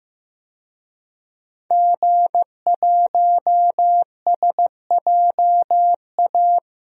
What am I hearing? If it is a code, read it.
Morse code, heard as a keyed tone, about 15 wpm, 700 Hz. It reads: G1SJA
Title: DXCC-100 Call Signs - 5 Characters - Repeated 2x - 15wpm